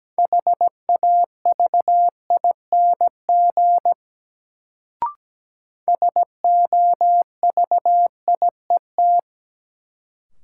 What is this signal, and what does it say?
Morse code, heard as a keyed tone, about 17 wpm, 700 Hz. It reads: HAVING SOVIET
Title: Sets of 5 Words Encore - New Word Drill - 17wpm